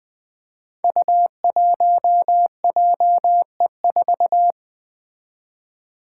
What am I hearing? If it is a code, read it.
Morse code, heard as a keyed tone, about 20 wpm, 700 Hz. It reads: U1JE4
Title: Five Challenging Letter-Number Combinations 20wpm